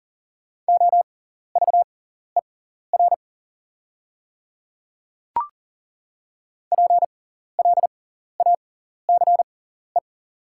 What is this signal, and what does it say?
Morse code, heard as a keyed tone, about 40 wpm, spaced as for 14 wpm, 700 Hz. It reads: OVER PLACE
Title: 6x Character Spacing - Top 200 Words - 40wpm